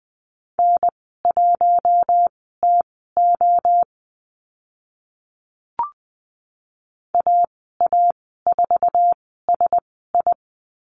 Morse code, heard as a keyed tone, about 20 wpm, spaced as for 15 wpm, 700 Hz. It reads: N1TO AA4SI
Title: 062 Lesson 11g - Letter h - Call Signs 20wpm